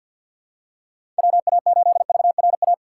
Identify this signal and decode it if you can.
Morse code, heard as a keyed tone, about 51 wpm, 700 Hz. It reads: WA8VRA
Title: USA Call Signs - 6 Characters - 6x Character Spacing - 50wpm